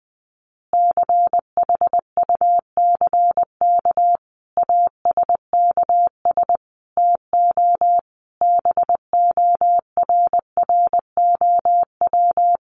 Morse code, heard as a keyed tone, about 20 wpm, 700 Hz. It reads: CHUCK ASKS TO BORROW
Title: Additional Easy Phrases 20wpm